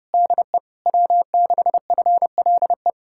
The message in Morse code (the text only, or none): DE W6FLE